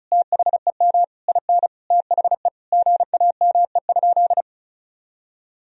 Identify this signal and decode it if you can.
Morse code, heard as a keyed tone, about 36 wpm, 700 Hz. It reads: THEM IN THE GAME?
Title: Sentences from Top 1000 Words - Review Part 4 - 35wpm